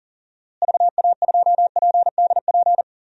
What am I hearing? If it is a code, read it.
Morse code, heard as a keyed tone, about 40 wpm, 700 Hz. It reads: VA2PDP
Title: DXCC-100 Call Signs - 6 Characters - Repeated 2x - 40wpm